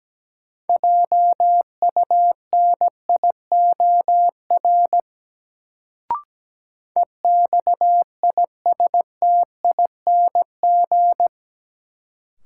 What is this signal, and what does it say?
Morse code, heard as a keyed tone, about 17 wpm, 700 Hz. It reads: JUNIOR EXISTING
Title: Sentences from Top 1500 Words - New Word Drill Alt - 17wpm